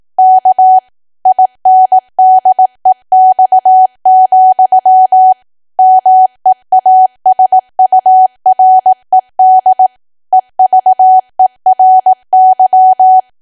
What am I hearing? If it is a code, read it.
Morse code, heard as a keyed tone, about 18 wpm, 750 Hz. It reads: K INDEX, MEASURED EVERY
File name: cwa / W1AW-Level2 / 18WPM